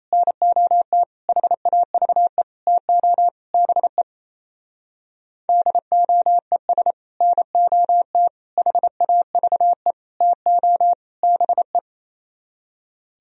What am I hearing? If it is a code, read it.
Morse code, heard as a keyed tone, about 38 wpm, 700 Hz. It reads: NOT HAVE TO BE DOES NOT HAVE TO BE